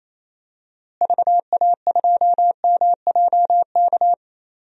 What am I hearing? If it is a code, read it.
Morse code, heard as a keyed tone, about 27 wpm, 700 Hz. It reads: VA2MJK
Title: DXCC-100 Call Signs - 6 Characters - Repeated 3x - 28wpm